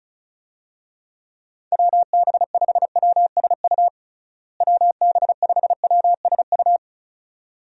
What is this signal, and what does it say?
Morse code, heard as a keyed tone, about 34 wpm, 700 Hz. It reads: WB5WSU WB5WSU
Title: USA Call Signs - 6 Characters - Repeated 2x - 35wpm